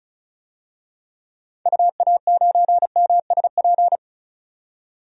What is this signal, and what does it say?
Morse code, heard as a keyed tone, about 36 wpm, 700 Hz. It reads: UA9MSP
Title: DXCC-100 Call Signs - 6 Characters - 35wpm